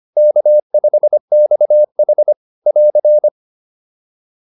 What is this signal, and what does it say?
Morse code, heard as a keyed tone, about 25 wpm, 590 Hz. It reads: K5XH <AR>
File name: POTA205a 25